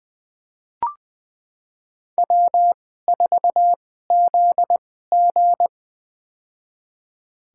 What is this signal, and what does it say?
Morse code, heard as a keyed tone, about 21 wpm, spaced as for 15 wpm, 700 Hz. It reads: W4ZG
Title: USA Call Signs - 4 Characters - 2x Character Spacing - 20wpm